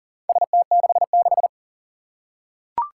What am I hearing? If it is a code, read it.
Morse code, heard as a keyed tone, about 44 wpm, 690 Hz. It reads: ST66
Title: Four Challenging Letter-Number Combinations 40wpm